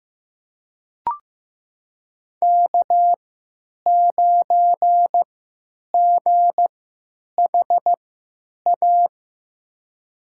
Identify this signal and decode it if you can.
Morse code, heard as a keyed tone, about 15 wpm, spaced as for 9 wpm, 700 Hz. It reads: K9GHA
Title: USA Call Signs - 5 Characters - 3x Character Spacing - 15wpm